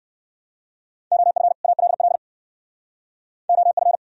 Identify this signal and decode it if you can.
Morse code, heard as a keyed tone, about 68 wpm, 700 Hz. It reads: OF ALL OF